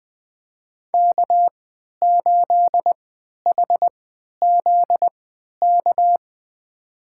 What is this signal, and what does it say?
Morse code, heard as a keyed tone, about 20 wpm, spaced as for 11 wpm, 700 Hz. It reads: K8HZK
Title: USA Call Signs - 3x Character Spacing - 20wpm